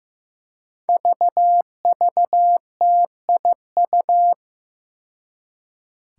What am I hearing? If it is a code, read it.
Morse code, heard as a keyed tone, about 16 wpm, 700 Hz. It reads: VVTIU
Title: Five Challenging Letter-Number Combinations 15wpm